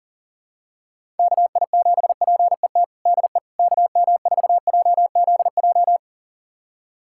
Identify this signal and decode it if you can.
Morse code, heard as a keyed tone, about 40 wpm, 700 Hz. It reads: KI7PET DE KM4JZJ
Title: USA Callsign DE Callsign - 6 Characters - 40wpm